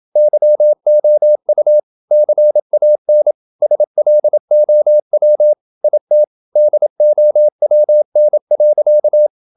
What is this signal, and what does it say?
Morse code, heard as a keyed tone, about 27 wpm, 600 Hz. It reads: YOU CAN SLOW IT DOWN.
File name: Never 0027wpm